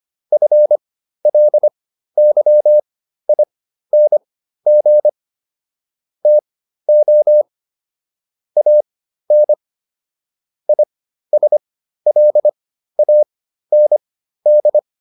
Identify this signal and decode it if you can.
Morse code, heard as a keyed tone, about 25 wpm, spaced as for 13 wpm, 600 Hz. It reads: FLYING TO AN ISLAND